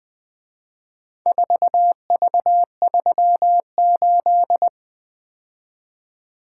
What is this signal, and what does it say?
Morse code, heard as a keyed tone, about 22 wpm, 700 Hz. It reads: 4V38
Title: Speed-Racing - Four Letter-Number Combinations - 20wpm